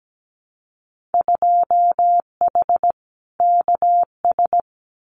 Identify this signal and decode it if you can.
Morse code, heard as a keyed tone, about 17 wpm, 700 Hz. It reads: 2H KS
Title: Winter Field Day Exchanges Encore 17wpm